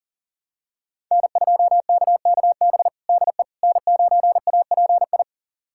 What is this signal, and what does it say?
Morse code, heard as a keyed tone, about 40 wpm, 690 Hz. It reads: N2KKB DE N9API
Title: USA Callsign DE Callsign - 5 Characters - 40wpm